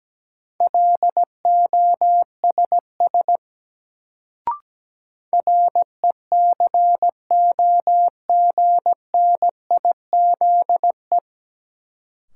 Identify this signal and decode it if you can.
Morse code, heard as a keyed tone, about 17 wpm, 700 Hz. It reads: LOSS RECOGNIZE
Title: Sentences from Top 1400 Words - New Word Drill - 17wpm